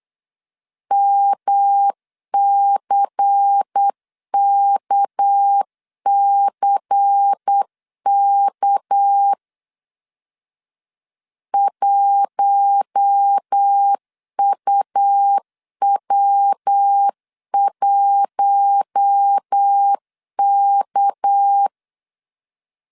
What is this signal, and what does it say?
Morse code, heard as a keyed tone, about 8 wpm, 800 Hz. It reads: MCKCK 1UW1K
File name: LEZIONI / 6 PKX0 / 6 PK 8wpm 3003